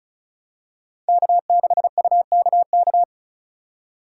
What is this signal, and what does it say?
Morse code, heard as a keyed tone, about 36 wpm, 700 Hz. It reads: K6UKK
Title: USA Call Signs - 5 Characters - 2x Character Spacing - 35wpm